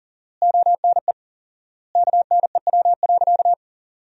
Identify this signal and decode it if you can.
Morse code, heard as a keyed tone, about 40 wpm, 700 Hz. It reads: ONE KNEW.
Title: 4x Word-Spacing - Sentences from Top 400 Words - 40wpm